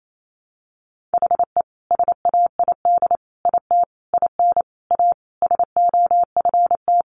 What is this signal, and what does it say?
Morse code, heard as a keyed tone, about 28 wpm, 700 Hz. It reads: HE SAID IT IN A SOFT
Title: Sentences from Top 1500 Words 28wpm